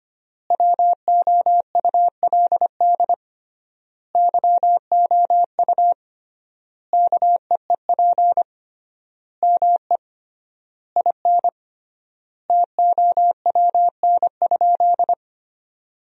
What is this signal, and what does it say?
Morse code, heard as a keyed tone, about 25 wpm, 700 Hz. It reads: WOULD YOU KEEP ME IN TOWN?
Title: 3x Word-Spacing - Sentences from Top 500 Words - 25wpm